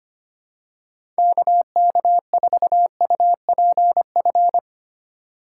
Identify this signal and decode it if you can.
Morse code, heard as a keyed tone, about 25 wpm, 700 Hz. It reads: KK4UPF
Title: USA Call Signs - 6 Characters - 3x Character Spacing - 25wpm